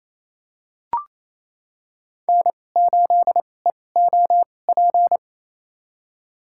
Morse code, heard as a keyed tone, about 28 wpm, spaced as for 20 wpm, 700 Hz. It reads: N8EOP
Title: USA Call Signs - 2x Character Spacing - 28wpm